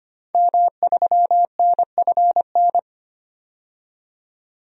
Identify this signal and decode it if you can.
Morse code, heard as a keyed tone, about 25 wpm, 700 Hz. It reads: M3NFN
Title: DXCC-100 Call Signs - 5 Characters - 25wpm